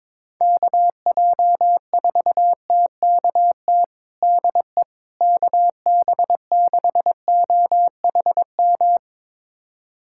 Morse code, heard as a keyed tone, about 22 wpm, 700 Hz. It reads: KJ4TKT DE KB6OHM
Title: USA Callsign DE Callsign 22wpm